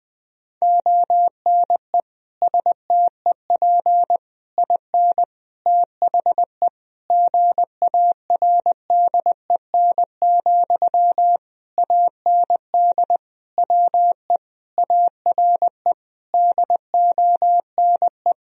Morse code, heard as a keyed tone, about 20 wpm, 700 Hz. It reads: ONE STEP IN THE GARDEN, AND WE ARE DONE
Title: Sentences from Top 1000 Words - Review Part 4 - 20wpm